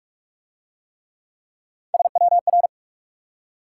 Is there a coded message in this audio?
Yes: Morse code from 1.9 to 2.7 s, heard as a keyed tone, about 44 wpm, 700 Hz. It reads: SWR